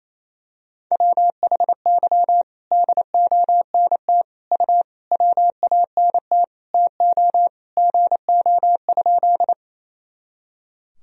Morse code, heard as a keyed tone, about 28 wpm, 700 Hz. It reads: WHY DONT U WANT TO GO?
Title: Sentences from Top 300 Words and Abbreviations 28wpm